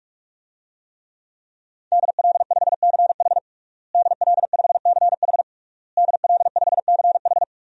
Morse code, heard as a keyed tone, about 44 wpm, 700 Hz. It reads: DL5CH DL5CH DL5CH
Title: DXCC-100 Call Signs - 5 Characters - Repeated 3x - 45wpm